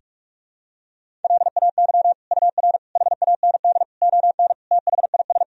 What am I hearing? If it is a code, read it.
Morse code, heard as a keyed tone, about 46 wpm, 700 Hz. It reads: LAY UR HAND ON THIS
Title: Sentences from Top 400 Words and Abbreviations 45wpm